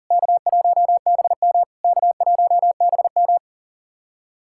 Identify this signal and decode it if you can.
Morse code, heard as a keyed tone, about 40 wpm, 690 Hz. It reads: K1BM K1BM
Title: SOTA QSO Practice 40wpm